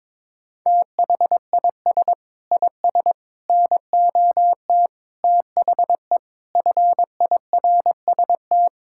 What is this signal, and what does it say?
Morse code, heard as a keyed tone, about 22 wpm, 700 Hz. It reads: THIS IS NOT THE FIRST
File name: practice / 22wpm / Sets of 8 Words - 22wpm